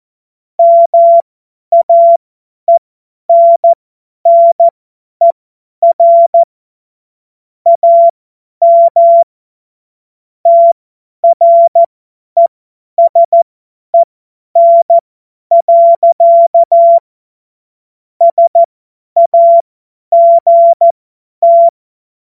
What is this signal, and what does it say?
Morse code, heard as a keyed tone, about 14 wpm, spaced as for 10 wpm, 680 Hz. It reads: MAENNER AM TRESEN. SAGT
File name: Morsekurs / Witze / Witz 22. Februar 2022 50 F70